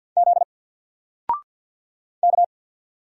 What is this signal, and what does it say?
Morse code, heard as a keyed tone, about 50 wpm, 700 Hz. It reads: Z K
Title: Single Letter-Number Alt 50wpm